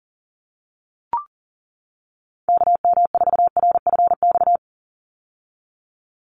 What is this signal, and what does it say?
Morse code, heard as a keyed tone, about 39 wpm, 700 Hz. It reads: KM4RFX
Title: USA Call Signs 40wpm